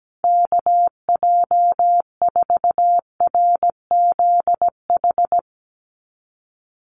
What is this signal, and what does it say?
Morse code, heard as a keyed tone, about 17 wpm, 700 Hz. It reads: KJ4RZH
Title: USA Call Signs 17wpm